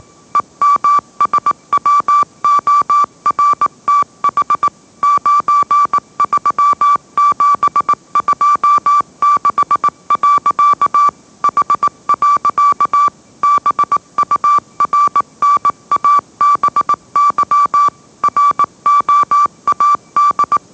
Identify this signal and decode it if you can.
Morse code, heard as a keyed tone, about 18 wpm, 1160 Hz. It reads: EMSWORTH 93726. H. BURNABY ROAD
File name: mrx / mrx082 20wpm